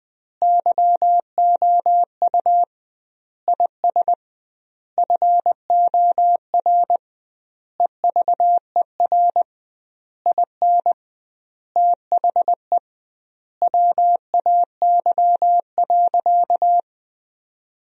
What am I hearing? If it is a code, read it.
Morse code, heard as a keyed tone, about 21 wpm, 700 Hz. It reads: YOU IS FOR EVER IN THE WAY.
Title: 2x Word-Spacing - Sentences from Top 400 Words - 20wpm